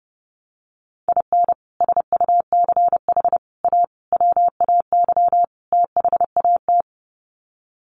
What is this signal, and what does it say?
Morse code, heard as a keyed tone, about 30 wpm, 700 Hz. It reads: IN SUCH A WAY THAT
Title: Sets of 5 Words 30wpm